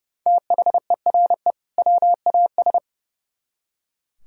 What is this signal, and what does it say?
Morse code, heard as a keyed tone, about 37 wpm, 700 Hz. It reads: THERE WAS
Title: Speed-Racing - Sets of 2 Words 30wpm